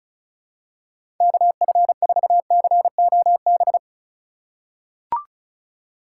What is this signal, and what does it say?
Morse code, heard as a keyed tone, about 36 wpm, 700 Hz. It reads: KF4COB
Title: USA Call Signs - 6 Characters - 6x Character Spacing - 35wpm